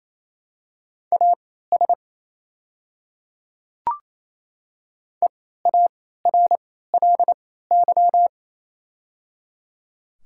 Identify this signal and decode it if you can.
Morse code, heard as a keyed tone, about 28 wpm, spaced as for 16 wpm, 690 Hz. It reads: AS EARLY
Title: 3x Character Spacing - Top 500 Words - 28wpm